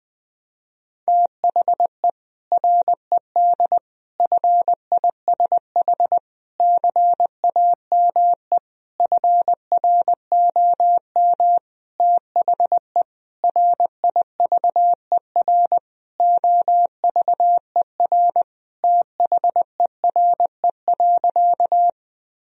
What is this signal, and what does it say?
Morse code, heard as a keyed tone, about 20 wpm, 700 Hz. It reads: THE RED FISH CAME FROM THE RIVER OVER THERE.
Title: Sentences from Top 1000 Words - Review Part 2 - 20wpm